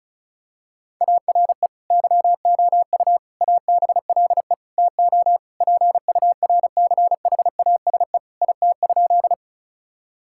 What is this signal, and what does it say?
Morse code, heard as a keyed tone, about 36 wpm, 700 Hz. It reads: ARE YOU ABLE TO PURCHASE IT?
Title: Sentences from Top 1000 Words - Review Part 5 - 35wpm